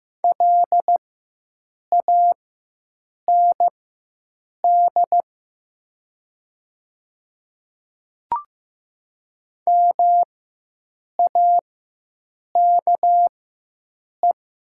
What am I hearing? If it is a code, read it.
Morse code, heard as a keyed tone, about 15 wpm, spaced as for 7 wpm, 700 Hz. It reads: LAND MAKE
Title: 4x Character Spacing - Top 300 Words - 15wpm